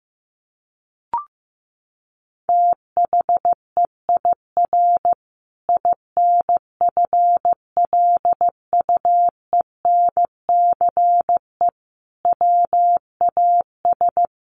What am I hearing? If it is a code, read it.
Morse code, heard as a keyed tone, about 15 wpm, 700 Hz. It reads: THEIR INFLUENCE WAS
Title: Sentences from Top 1400 Words 15wpm